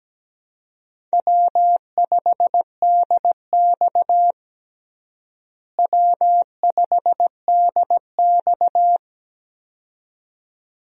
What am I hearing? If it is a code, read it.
Morse code, heard as a keyed tone, about 17 wpm, 700 Hz. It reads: W5DX W5DX
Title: USA Call Signs - 4 Characters - Repeated 2x - 17wpm